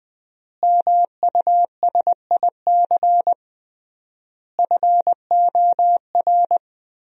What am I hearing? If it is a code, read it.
Morse code, heard as a keyed tone, about 20 wpm, 700 Hz. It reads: MUSIC FOR
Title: 3x Word-Spacing - Sentences from Top 300 Words - 20wpm